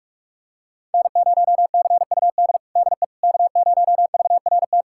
Morse code, heard as a keyed tone, about 41 wpm, 700 Hz. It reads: N0CUD DE K0VRT
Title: USA Callsign DE Callsign - 5 Characters - 45wpm